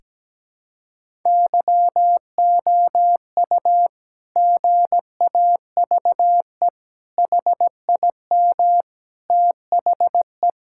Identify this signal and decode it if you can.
Morse code, heard as a keyed tone, about 17 wpm, 700 Hz. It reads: YOU GAVE HIM THE
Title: Sentences from Top 1000 Words - Review Part 6 - 17wpm